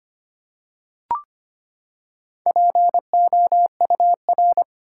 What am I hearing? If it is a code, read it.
Morse code, heard as a keyed tone, about 25 wpm, 700 Hz. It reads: POUR
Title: Four Letter Words - 25wpm